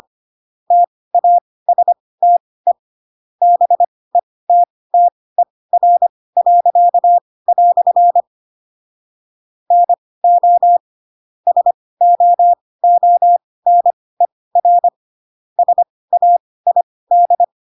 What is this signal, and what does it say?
Morse code, heard as a keyed tone, about 25 wpm, spaced as for 18 wpm, 700 Hz. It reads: TASTE BETTER." NO SOONER SAID
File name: Pinocchio / Pinocchio ch5 12 25 ik1hge